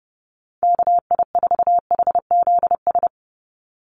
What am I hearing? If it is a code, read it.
Morse code, heard as a keyed tone, about 30 wpm, 700 Hz. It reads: KI4HZS
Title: USA Call Signs 30wpm